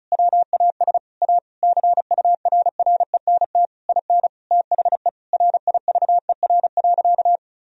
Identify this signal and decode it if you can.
Morse code, heard as a keyed tone, about 36 wpm, 700 Hz. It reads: WAS A CURRENT IN THE RIVER.
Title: Sentences from Top 1000 Words - Review Part 4 - 35wpm